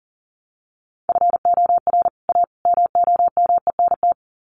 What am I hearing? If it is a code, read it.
Morse code, heard as a keyed tone, about 40 wpm, 700 Hz. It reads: FOR A MOMENT
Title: Sets of 3 Words 40wpm